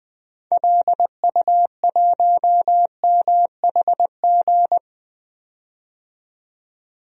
Morse code, heard as a keyed tone, about 20 wpm, 700 Hz. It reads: LU1MHG
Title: DXCC-100 Call Signs - 6 Characters - 20wpm